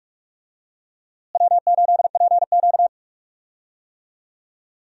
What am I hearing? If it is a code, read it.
Morse code, heard as a keyed tone, about 45 wpm, 700 Hz. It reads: W8PQ
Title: USA Call Signs - 4 Characters - Repeated 2x - 45wpm